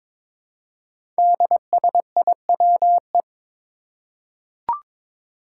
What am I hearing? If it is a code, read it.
Morse code, heard as a keyed tone, about 22 wpm, 700 Hz. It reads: DSIWE
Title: Five Challenging Letter-Number Combinations 22wpm